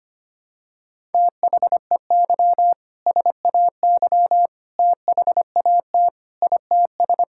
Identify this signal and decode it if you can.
Morse code, heard as a keyed tone, about 25 wpm, 700 Hz. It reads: THEY SAY THAT ITS
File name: practice / 25wpm / Sentences with 2-Word Phrases - 25wpm